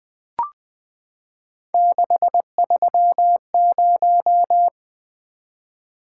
Morse code, heard as a keyed tone, about 20 wpm, 700 Hz. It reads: 630
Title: Three Letter-Number Combinations - 20wpm